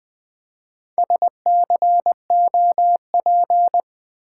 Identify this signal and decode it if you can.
Morse code, heard as a keyed tone, about 20 wpm, 700 Hz. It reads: SCOP
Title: Build A Word - 1 Letter At A Time - 5-Letter Words - 20wpm